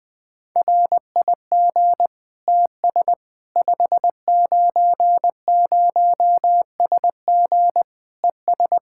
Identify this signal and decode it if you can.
Morse code, heard as a keyed tone, about 20 wpm, 700 Hz. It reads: RIG TS 590SG ES
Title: Practice QSO - Exchange 2 - Info - 20wpm